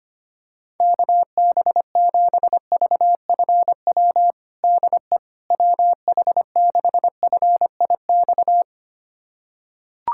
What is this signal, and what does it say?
Morse code, heard as a keyed tone, about 25 wpm, 700 Hz. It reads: KB7VFW DE WH6FIX
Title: USA Callsign DE Callsign - 6 Characters - 25wpm